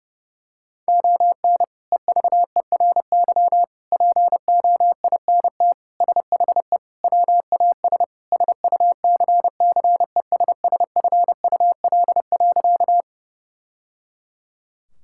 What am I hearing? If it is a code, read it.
Morse code, heard as a keyed tone, about 32 wpm, 700 Hz. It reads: ON EVERY POINT SHE WAS SUCCESSFUL.
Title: Sentences from Top 1000 Words - Review Part 6 - 30wpm